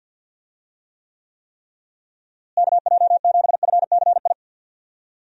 Morse code, heard as a keyed tone, about 47 wpm, 700 Hz. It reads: KJ7FCI
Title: USA Call Signs - 6 Characters - Repeated 3x - 50wpm